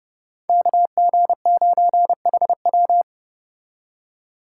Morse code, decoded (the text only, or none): KG9HW